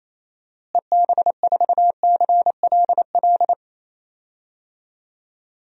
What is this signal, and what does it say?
Morse code, heard as a keyed tone, about 28 wpm, 700 Hz. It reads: EB4CLL
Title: DXCC-100 Call Signs - 6 Characters - Repeated 3x - 28wpm